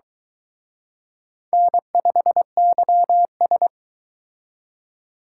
Morse code, heard as a keyed tone, about 22 wpm, 700 Hz. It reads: N5YS